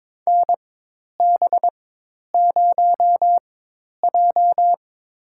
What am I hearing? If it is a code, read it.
Morse code, heard as a keyed tone, about 22 wpm, spaced as for 10 wpm, 700 Hz. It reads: NB0J
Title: USA Call Signs - 4 Characters - 4x Character Spacing - 22wpm